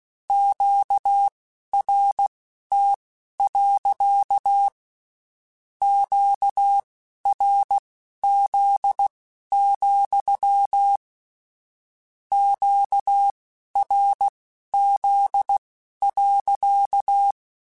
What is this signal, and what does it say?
Morse code, heard as a keyed tone, about 16 wpm, spaced as for 12 wpm, 800 Hz. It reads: QRT. QRZ, QRZ.
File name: Q Calls / Q Calls-mp3s / Q Calls sppc @15